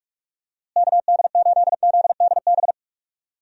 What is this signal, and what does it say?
Morse code, heard as a keyed tone, about 45 wpm, 700 Hz. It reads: KD8ZDB
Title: USA Call Signs - 6 Characters - Repeated 3x - 45wpm